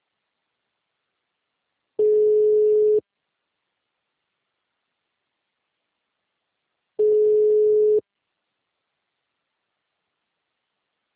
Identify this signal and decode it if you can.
Ringback tone (Europe, 425 Hz).
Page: TLMK - Rio de janeiro 2